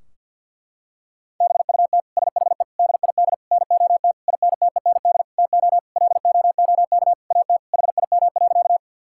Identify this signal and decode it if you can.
Morse code, heard as a keyed tone, about 50 wpm, 700 Hz. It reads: BUT SHE DID NOT INTEND TO LOOK AT HIM.
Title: Sentences from Top 1600 Words 50wpm